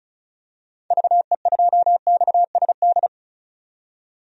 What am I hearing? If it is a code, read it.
Morse code, heard as a keyed tone, about 36 wpm, 700 Hz. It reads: VE2XSD
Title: DXCC-100 Call Signs - 6 Characters - 35wpm